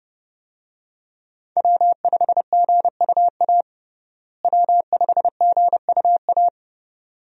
Morse code, heard as a keyed tone, about 30 wpm, 700 Hz. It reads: W5GUA W5GUA
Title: USA Call Signs - 5 Characters - Repeated 2x - 30wpm